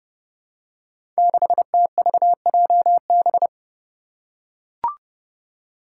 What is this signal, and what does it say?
Morse code, heard as a keyed tone, about 30 wpm, 700 Hz. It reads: 6TVJB
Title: Five Challenging Letter-Number Combinations 30wpm